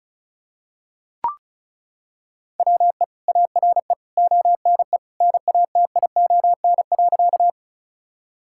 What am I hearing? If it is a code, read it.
Morse code, heard as a keyed tone, about 34 wpm, 700 Hz. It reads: WE ARE ONE NATION.
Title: Sentences from Top 1000 Words - Review Part 6 - 35wpm